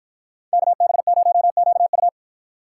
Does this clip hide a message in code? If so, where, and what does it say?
Yes: Morse code from 0.5 to 2.1 s, heard as a keyed tone, about 53 wpm, 700 Hz. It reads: KB0QU